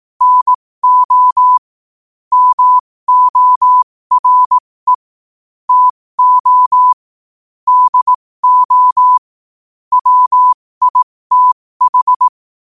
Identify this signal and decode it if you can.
Morse code, heard as a keyed tone, about 18 wpm, spaced as for 15 wpm, 1000 Hz. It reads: NO MORE TO DO WITH